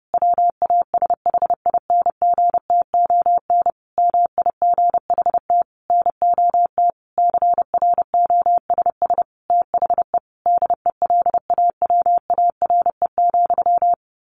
Morse code, heard as a keyed tone, about 30 wpm, 700 Hz. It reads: WASHINGTON MIGHT NOT CROSS THE DELAWARE,
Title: A Sound Of Thunder 30wpm